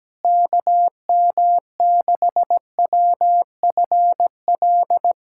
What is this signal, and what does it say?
Morse code, heard as a keyed tone, about 17 wpm, 700 Hz. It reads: KM6WFL
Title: USA Callsign DE Callsign - 6 Characters - 17wpm